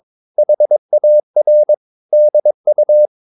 Morse code, heard as a keyed tone, about 22 wpm, 600 Hz. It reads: HAR DU
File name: Ferie QRS 22wpm